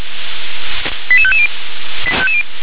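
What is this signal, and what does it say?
Five-tone selective call (ZVEI), heard as quick runs of five tones, tones of about 70 ms. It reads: 88550
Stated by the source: ZVEI five-tone call received on SDR.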